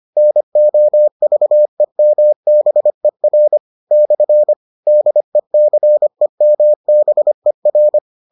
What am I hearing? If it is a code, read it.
Morse code, heard as a keyed tone, about 25 wpm, 600 Hz. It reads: NOVEMBER / DECEMBER